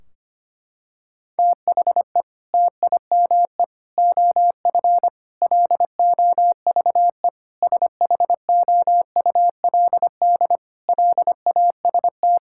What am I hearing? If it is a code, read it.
Morse code, heard as a keyed tone, about 25 wpm, 700 Hz. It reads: THE TIME OF LOVE SHOULD LAST
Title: Sentences from Top 1600 Words 25wpm